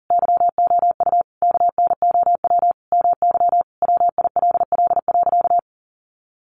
Morse code, heard as a keyed tone, about 40 wpm, 700 Hz. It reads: YOU KNOW MY WILL.
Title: Sentences from Top 1500 Words 40wpm